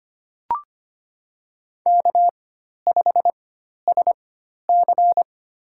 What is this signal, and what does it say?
Morse code, heard as a keyed tone, about 25 wpm, spaced as for 12 wpm, 700 Hz. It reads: K5SC
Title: USA Call Signs - 4 Characters - 4x Character Spacing - 25wpm